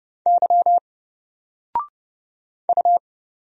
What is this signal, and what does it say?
Morse code, heard as a keyed tone, about 30 wpm, 700 Hz. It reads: Y U
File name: Single-Letter-Number-30wpm